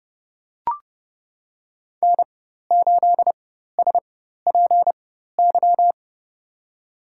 Morse code, heard as a keyed tone, about 30 wpm, spaced as for 14 wpm, 700 Hz. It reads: N8SPY
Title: USA Call Signs - 5 Characters - 4x Character Spacing - 30wpm